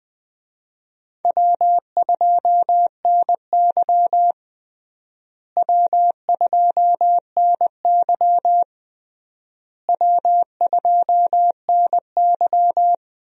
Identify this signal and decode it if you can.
Morse code, heard as a keyed tone, about 20 wpm, 700 Hz. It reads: W2NY W2NY W2NY
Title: USA Call Signs - 4 Characters - Repeated 3x - 20wpm